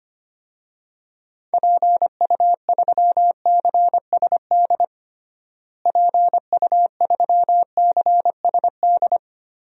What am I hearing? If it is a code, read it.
Morse code, heard as a keyed tone, about 25 wpm, 700 Hz. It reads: PU3CSD PU3CSD
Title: DXCC-100 Call Signs - 6 Characters - Repeated 2x - 25wpm